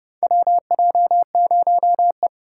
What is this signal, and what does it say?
Morse code, heard as a keyed tone, about 31 wpm, 700 Hz. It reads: WJ0E
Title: USA Call Signs - 4 Characters - 4x Character Spacing - 30wpm